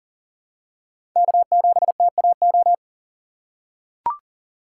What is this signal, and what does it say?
Morse code, heard as a keyed tone, about 40 wpm, 700 Hz. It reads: K7TAO
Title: USA Call Signs - 5 Characters - 3x Character Spacing - 40wpm